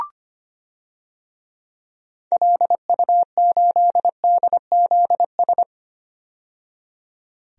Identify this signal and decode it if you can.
Morse code, heard as a keyed tone, about 25 wpm, 700 Hz. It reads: LU8DZS
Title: DXCC-100 Call Signs - 6 Characters - 25wpm